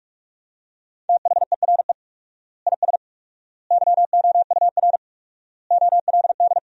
Morse code, heard as a keyed tone, about 42 wpm, 700 Hz. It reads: THERE IS YOUR OLD
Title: 4x Word-Spacing - Sentences from Top 500 Words - 45wpm